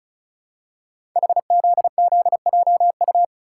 Morse code, heard as a keyed tone, about 36 wpm, 700 Hz. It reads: HZZJU
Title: Code Groups - 5 Letters - 35wpm